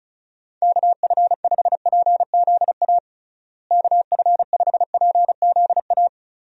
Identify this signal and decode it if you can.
Morse code, heard as a keyed tone, about 34 wpm, 700 Hz. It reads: KF5PZA KF5PZA
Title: USA Call Signs - 6 Characters - Repeated 3x - 35wpm